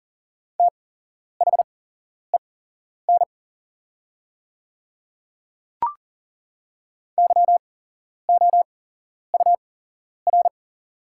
Morse code, heard as a keyed tone, about 40 wpm, spaced as for 11 wpm, 700 Hz. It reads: THEN YOUR
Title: 8x Character Spacing - Top 100 Words - 40wpm